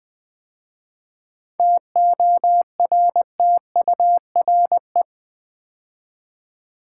Morse code, heard as a keyed tone, about 20 wpm, 700 Hz. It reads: TORTURE
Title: 7 Letter Words - First 3 Letters Given - 20wpm